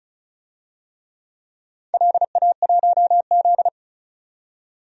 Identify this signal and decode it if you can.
Morse code, heard as a keyed tone, about 36 wpm, 690 Hz. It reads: LA1Z
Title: DXCC-100 Call Signs - 4 Characters - 35wpm